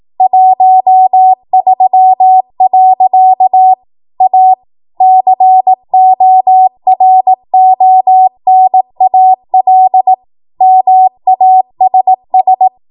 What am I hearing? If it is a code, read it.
Morse code, heard as a keyed tone, about 18 wpm, 750 Hz. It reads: 13. A CORONAL MASS
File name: Morse / 18WPM